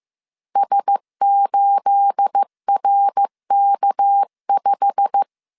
Morse code, heard as a keyed tone, about 15 wpm, 800 Hz. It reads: S8RK5
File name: CW / DET / 1 9 1500C S A 14wpm 3222